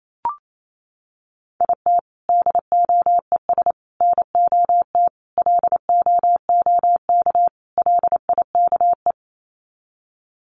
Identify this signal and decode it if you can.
Morse code, heard as a keyed tone, about 28 wpm, 700 Hz. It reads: IT DOES NOT LOOK LIKE
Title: Sets of 5 Words Encore2 28wpm